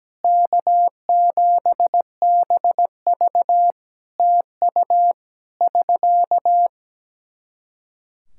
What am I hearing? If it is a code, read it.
Morse code, heard as a keyed tone, about 17 wpm, 700 Hz. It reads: K7BV TU <SK>
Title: K1USN SST QSO Practice 17wpm